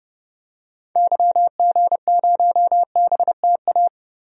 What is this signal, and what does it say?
Morse code, heard as a keyed tone, about 30 wpm, 700 Hz. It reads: YG0BTA
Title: DXCC-100 Call Signs - 6 Characters - 30wpm